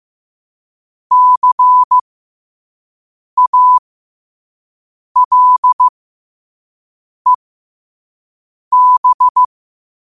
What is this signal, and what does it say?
Morse code, heard as a keyed tone, about 15 wpm, spaced as for 5 wpm, 1000 Hz. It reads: CALEB
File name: qotdfiles / 5wpm